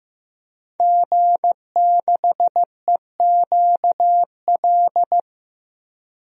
Morse code, heard as a keyed tone, about 15 wpm, 700 Hz. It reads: G6EQL
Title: DXCC-100 Call Signs - 5 Characters - Repeated 3x - 15wpm